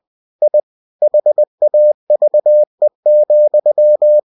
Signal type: Morse code